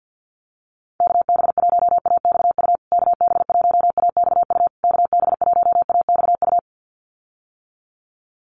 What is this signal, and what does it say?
Morse code, heard as a keyed tone, about 50 wpm, 700 Hz. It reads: KB1AXU KB1AXU KB1AXU
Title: USA Call Signs Repeated 3x 50wpm